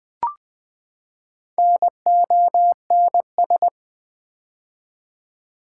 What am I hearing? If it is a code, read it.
Morse code, heard as a keyed tone, about 20 wpm, 700 Hz. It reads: NONS
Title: Build A Word - 4 Letters At A Time - 8-Letter Words - 20wpm